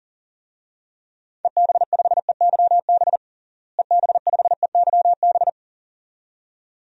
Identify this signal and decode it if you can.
Morse code, heard as a keyed tone, about 40 wpm, 700 Hz. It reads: EB5EYB EB5EYB
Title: DXCC-100 Call Signs - 6 Characters - Repeated 2x - 40wpm